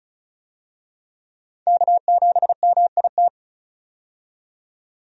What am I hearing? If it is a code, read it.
Morse code, heard as a keyed tone, about 36 wpm, 700 Hz. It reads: K7MIT
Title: USA Call Signs - 5 Characters - 35wpm